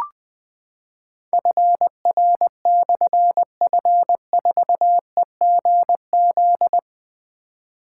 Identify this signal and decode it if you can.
Morse code, heard as a keyed tone, about 20 wpm, 700 Hz. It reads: FR/F4EGZ
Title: Difficult Call Signs 20wpm